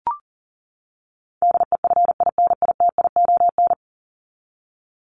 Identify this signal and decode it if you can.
Morse code, heard as a keyed tone, about 40 wpm, 700 Hz. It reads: DEFINITION
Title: Top 501-750 Words 40wpm